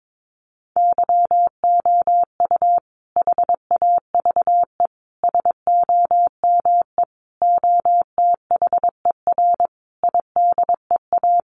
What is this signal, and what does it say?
Morse code, heard as a keyed tone, about 22 wpm, 700 Hz. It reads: YOU HAVE SOME OTHER IDEA
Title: Sentences from Top 1300 Words 22wpm